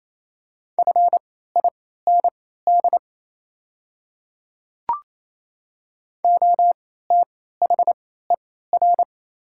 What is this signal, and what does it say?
Morse code, heard as a keyed tone, about 27 wpm, spaced as for 16 wpm, 690 Hz. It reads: FIND OTHER
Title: 3x Character Spacing - Top 100 Words - 28wpm